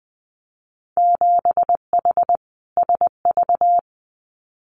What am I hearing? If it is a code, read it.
Morse code, heard as a keyed tone, about 19 wpm, 700 Hz. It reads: 7H SV
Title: Winter Field Day Exchanges Encore 20wpm